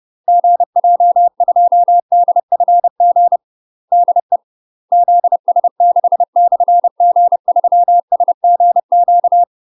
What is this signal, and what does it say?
Morse code, heard as a keyed tone, about 30 wpm, 700 Hz. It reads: GJ2DFG DE ZS6/G3SGQ
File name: UK MASTER QSOs TRAINING / 30WPM / UK MASTER QSOs 0020